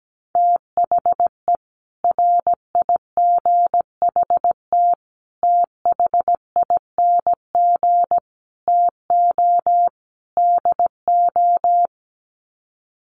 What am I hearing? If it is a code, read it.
Morse code, heard as a keyed tone, about 17 wpm, 700 Hz. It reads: THE RIGHT THING TO DO
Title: Sets of 5 Words 17wpm